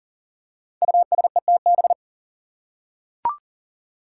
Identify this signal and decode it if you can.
Morse code, heard as a keyed tone, about 40 wpm, 700 Hz. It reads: USETB
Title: Five Challenging Letter-Number Combinations 40wpm